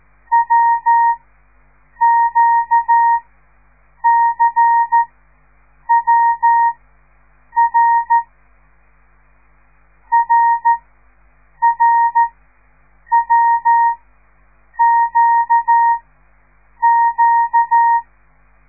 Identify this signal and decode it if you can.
Morse code, heard as a keyed tone, about 13 wpm, spaced as for 7 wpm, 930 Hz. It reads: WQCWR RRWQQ
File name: radiomorningstar / morse / les 5